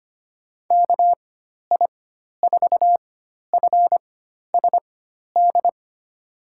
Morse code, heard as a keyed tone, about 25 wpm, spaced as for 12 wpm, 700 Hz. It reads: KI4FSD
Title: USA Call Signs - 6 Characters - 4x Character Spacing - 25wpm